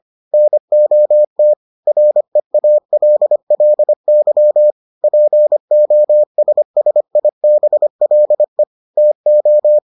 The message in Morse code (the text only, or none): NOT REALLY POSSIBLE TO